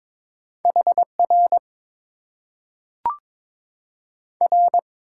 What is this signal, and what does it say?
Morse code, heard as a keyed tone, about 22 wpm, 700 Hz. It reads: HR R
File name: Top 90 QSO Elements - 6x Character Spacing - 22wpm